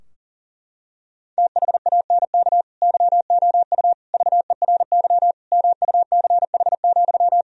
Morse code, heard as a keyed tone, about 40 wpm, 700 Hz. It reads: THANK YOU VERY MUCH,
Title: Sentences from Top 1000 Words - Review Part 4 - 40wpm